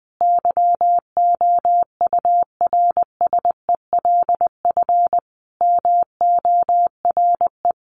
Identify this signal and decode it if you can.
Morse code, heard as a keyed tone, about 20 wpm, 700 Hz. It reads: YOURSELF MORE
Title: Sentences from Top 1300 Words 20wpm